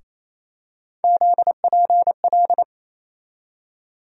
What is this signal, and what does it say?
Morse code, heard as a keyed tone, about 28 wpm, 700 Hz. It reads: ZPL
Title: USA Call Sign Postfixes - 3 Letters 28wpm